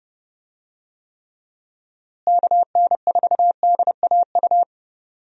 Morse code, heard as a keyed tone, about 30 wpm, 700 Hz. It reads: KN4DAU
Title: USA Call Signs - 6 Characters - Repeated 2x - 30wpm